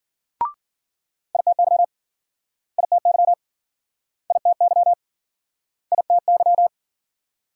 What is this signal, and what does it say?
Morse code, heard as a keyed tone, about 57 wpm, 700 Hz. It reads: ITY ITY ITY ITY